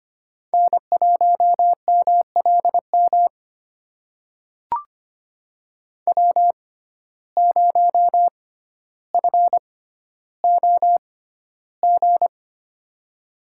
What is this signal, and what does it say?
Morse code, heard as a keyed tone, about 25 wpm, spaced as for 13 wpm, 700 Hz. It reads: N1MLM W0FOG
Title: USA Call Signs - 5 Characters - 6x Character Spacing - 25wpm